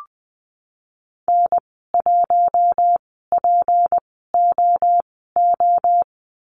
Morse code, heard as a keyed tone, about 20 wpm, spaced as for 15 wpm, 700 Hz. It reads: N1POO
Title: 173 Lesson 28g - Number 9 - Call Signs 20wpm